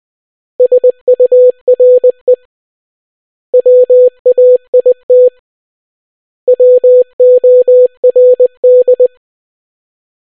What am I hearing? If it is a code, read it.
Morse code, heard as a keyed tone, about 20 wpm, 500 Hz. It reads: SURE WAIT WORD